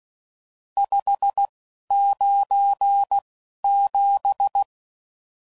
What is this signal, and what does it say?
Morse code, heard as a keyed tone, about 16 wpm, spaced as for 11 wpm, 800 Hz. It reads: 597
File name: CW Code Groups @10WPM / RSTs / Various RST No's